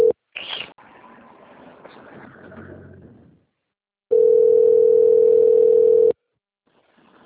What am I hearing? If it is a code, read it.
Ringback tone (North America), 440 + 480 Hz.
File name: out- - - - - .28611